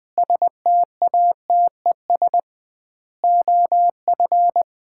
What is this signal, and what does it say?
Morse code, heard as a keyed tone, about 20 wpm, 700 Hz. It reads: STATES OF
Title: 2x Word-Spacing - Sets of 5 Words - 20wpm